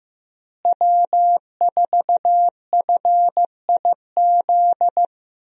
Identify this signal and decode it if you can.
Morse code, heard as a keyed tone, about 15 wpm, 700 Hz. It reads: W4FIZ
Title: USA Call Signs - 5 Characters - Repeated 2x - 15wpm